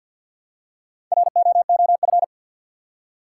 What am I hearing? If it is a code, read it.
Morse code, heard as a keyed tone, about 49 wpm, 700 Hz. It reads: ROOF
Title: Four Letter Words - 50wpm